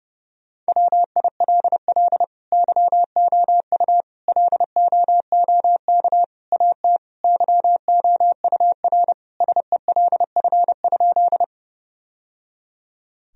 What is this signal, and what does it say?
Morse code, heard as a keyed tone, about 30 wpm, 700 Hz. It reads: WILL YOU LOOK AT YOUR SELF?
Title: Sentences from Top 1000 Words - Review Part 1 - 30wpm